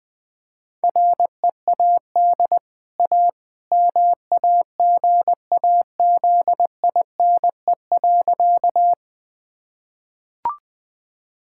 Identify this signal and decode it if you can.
Morse code, heard as a keyed tone, about 20 wpm, 700 Hz. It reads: READ A MAGAZINE.
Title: Easy Sentences with Hints - 20wpm